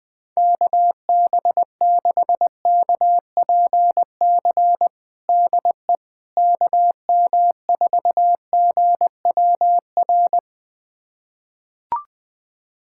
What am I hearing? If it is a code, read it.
Morse code, heard as a keyed tone, about 21 wpm, 700 Hz. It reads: KB6KPC DE KM4GWR
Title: USA Callsign DE Callsign - 6 Characters - 20wpm